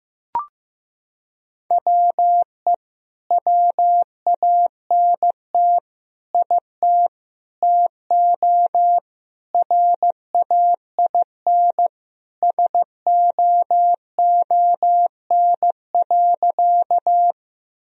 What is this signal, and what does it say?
Morse code, heard as a keyed tone, about 15 wpm, 700 Hz. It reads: WE WANT IT TO RAIN SOON.
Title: Sentences from Top 1000 Words - Review Part 1 - 15wpm